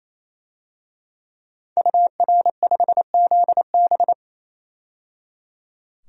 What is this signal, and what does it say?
Morse code, heard as a keyed tone, about 28 wpm, 700 Hz. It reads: UR5ZB
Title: DXCC-100 Call Signs - 5 Characters - 28wpm